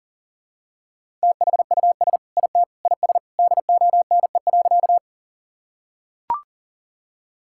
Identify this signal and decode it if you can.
Morse code, heard as a keyed tone, about 37 wpm, 700 Hz. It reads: THUS IT IS DONE.
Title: Sentences from Top 1000 Words - Review Part 5 - 40wpm